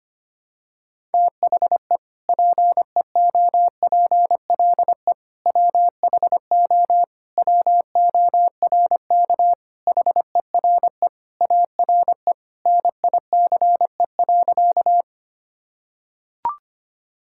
Morse code, heard as a keyed tone, about 25 wpm, 700 Hz. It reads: THE PEOPLE WHO WORK HERE ARE NICE.
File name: practice / 25wpm / Sentences with 3-Word Phrases - 25wpm